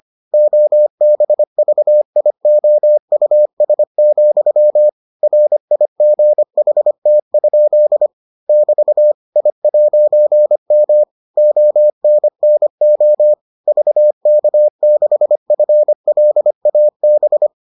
Morse code, heard as a keyed tone, about 25 wpm, 600 Hz. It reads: OBVIOUS, RIGHT? <BT> I'M ONNO VK6FLAB